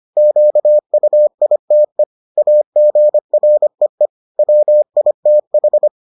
Morse code, heard as a keyed tone, about 25 wpm, 600 Hz. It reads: QUITE AGREE WITH